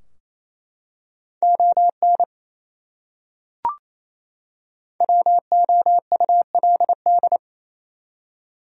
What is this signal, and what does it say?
Morse code, heard as a keyed tone, about 28 wpm, 700 Hz. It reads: ON WOULD
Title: Top 100 Words 28wpm